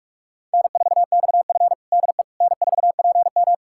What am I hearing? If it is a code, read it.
Morse code, heard as a keyed tone, about 46 wpm, 700 Hz. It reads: N4XF DE N4PM
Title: USA Callsign DE Callsign - 4 Characters - 45wpm